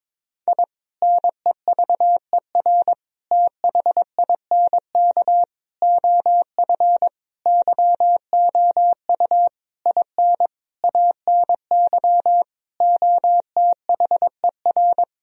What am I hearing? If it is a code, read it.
Morse code, heard as a keyed tone, about 22 wpm, 700 Hz. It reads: I NEVER THINK OF YOU IN ANY OTHER
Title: Sentences from Top 1000 Words - Review Part 1 - 22wpm